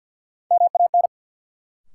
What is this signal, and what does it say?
Morse code, heard as a keyed tone, about 51 wpm, 700 Hz. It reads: MAN
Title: Sentences from Top 200 Words - New Word Drill - 50wpm